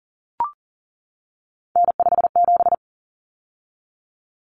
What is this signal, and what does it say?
Morse code, heard as a keyed tone, about 40 wpm, 700 Hz. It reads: N57
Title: Common Cut Numbers - 3-Digit 40wpm